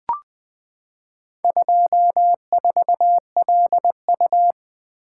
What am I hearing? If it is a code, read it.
Morse code, heard as a keyed tone, about 20 wpm, 700 Hz. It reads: 24LU
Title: Four Letter-Number Combinations - 20wpm